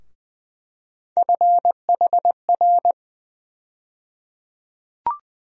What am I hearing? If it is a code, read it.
Morse code, heard as a keyed tone, about 20 wpm, 700 Hz. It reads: FHR
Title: USA Call Sign Postfixes - 3 Letters 20wpm